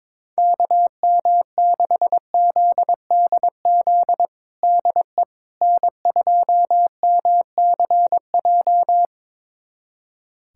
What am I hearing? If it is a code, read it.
Morse code, heard as a keyed tone, about 22 wpm, 700 Hz. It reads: KM6ZDZ DE N2MCJ
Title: USA Callsign DE Callsign 22wpm